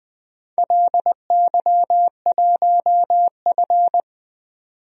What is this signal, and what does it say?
Morse code, heard as a keyed tone, about 20 wpm, 700 Hz. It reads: LY1F
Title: DXCC-100 Call Signs - 4 Characters - 20wpm